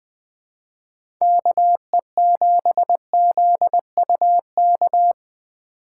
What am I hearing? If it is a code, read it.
Morse code, heard as a keyed tone, about 20 wpm, 700 Hz. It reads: KE7ZUK
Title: USA Call Signs - 6 Characters - 2x Character Spacing - 20wpm